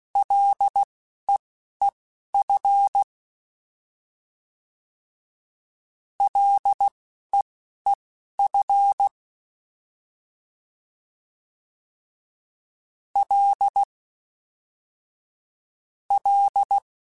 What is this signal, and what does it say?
Morse code, heard as a keyed tone, about 16 wpm, spaced as for 11 wpm, 800 Hz. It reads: LEEF LEEF L L
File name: F and L / F and L mp3 / F and L Prefixes 3x2 Twice @10